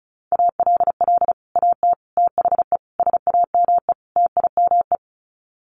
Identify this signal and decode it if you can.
Morse code, heard as a keyed tone, about 34 wpm, 700 Hz. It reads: ALL AT THE SAME TIME
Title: Sets of 5 Words Encore2 35wpm